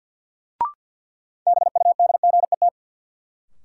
Morse code, heard as a keyed tone, about 53 wpm, 700 Hz. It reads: BUDGET
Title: Sentences from Top 1200 Words - New Word Drill - 50wpm